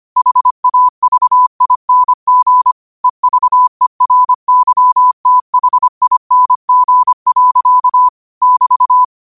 Morse code, heard as a keyed tone, about 25 wpm, 1000 Hz. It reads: HAVING EVERYTHING. <BT>
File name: qotdfiles / 25wpm